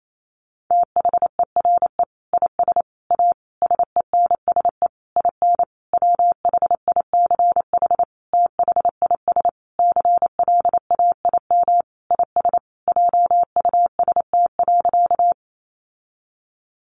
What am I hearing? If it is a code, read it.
Morse code, heard as a keyed tone, about 28 wpm, 700 Hz. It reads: THERE IS A SENSE IN WHICH THIS CLAIM IS JUST.
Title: Sentences from Top 1500 Words 28wpm